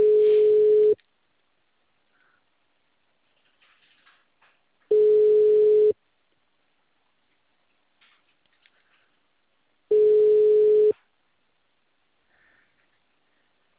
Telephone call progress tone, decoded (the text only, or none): ringback tone